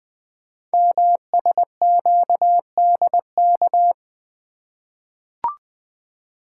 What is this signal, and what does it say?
Morse code, heard as a keyed tone, about 20 wpm, 700 Hz. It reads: MSQDK
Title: Code Groups - 5 Letters - 20wpm